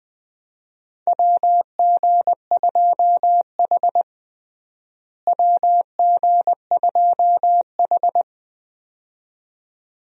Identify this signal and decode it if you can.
Morse code, heard as a keyed tone, about 20 wpm, 700 Hz. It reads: WG2H WG2H
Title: USA Call Signs - 4 Characters - Repeated 2x - 20wpm